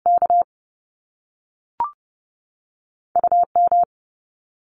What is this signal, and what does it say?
Morse code, heard as a keyed tone, about 30 wpm, 700 Hz. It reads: K U M
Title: Top Two Letter Combinations in Words 30wpm